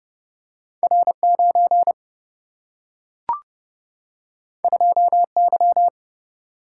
Morse code, heard as a keyed tone, about 30 wpm, 700 Hz. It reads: R9 2Y
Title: Single-Letter Single-Number 30wpm